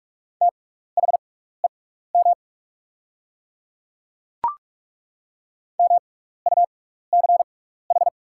Morse code, heard as a keyed tone, about 42 wpm, spaced as for 15 wpm, 690 Hz. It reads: THEM MUCH
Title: 6x Character Spacing - Top 300 Words - 45wpm